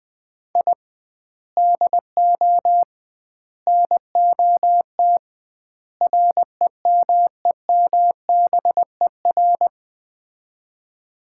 Morse code, heard as a keyed tone, about 20 wpm, 700 Hz. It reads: I DO NOT REMEMBER
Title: 2x Word-Spacing - Sets of 4 Words - 20wpm